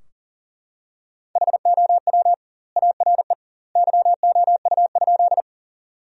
Morse code, heard as a keyed tone, about 40 wpm, 700 Hz. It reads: HOW ARE YOU?
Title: 2x Word-Spacing - Sentences from Top 100 Words - 40wpm